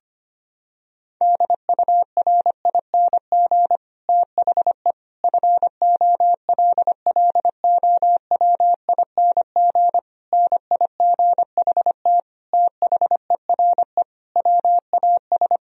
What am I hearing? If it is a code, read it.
Morse code, heard as a keyed tone, about 25 wpm, 700 Hz. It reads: DURING THE FOLLOWING NIGHT THERE WAS
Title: Sentences from Top 1000 Words - Review Part 4 - 25wpm